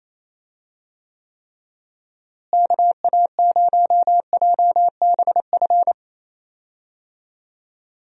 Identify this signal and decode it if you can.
Morse code, heard as a keyed tone, about 28 wpm, 700 Hz. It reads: KA0JBF
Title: USA Call Signs - 6 Characters - Repeated 3x - 28wpm